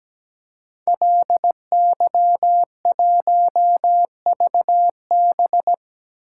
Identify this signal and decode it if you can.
Morse code, heard as a keyed tone, about 17 wpm, 700 Hz. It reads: LY1VB
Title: Code Groups - 5 Letter:Numbers - 17wpm